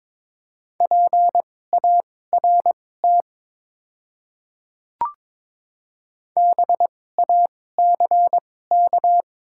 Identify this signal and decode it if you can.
Morse code, heard as a keyed tone, about 22 wpm, spaced as for 16 wpm, 700 Hz. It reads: PART BACK
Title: 2x Character Spacing - Top 200 Words - 22wpm